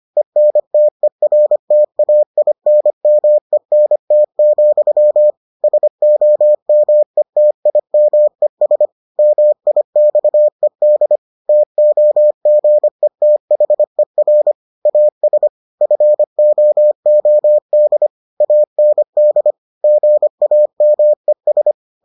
Morse code, heard as a keyed tone, about 25 wpm, 600 Hz. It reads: ENTERTAINMENT, SOMETIMES MIXED TOGETHER AS FOOD AND GAMES